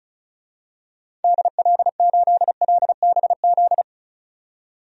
Morse code, heard as a keyed tone, about 36 wpm, 700 Hz. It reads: DL8LBZ
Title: DXCC-100 Call Signs - 6 Characters - 35wpm